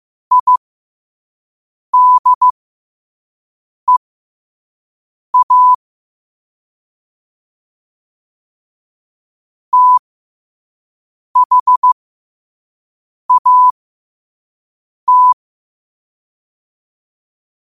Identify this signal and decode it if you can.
Morse code, heard as a keyed tone, about 15 wpm, spaced as for 5 wpm, 1000 Hz. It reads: IDEA THAT
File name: qotdfiles / 5wpm